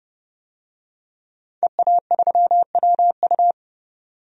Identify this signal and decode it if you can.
Morse code, heard as a keyed tone, about 30 wpm, 700 Hz. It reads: EA3WU
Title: DXCC-100 Call Signs - 5 Characters - 30wpm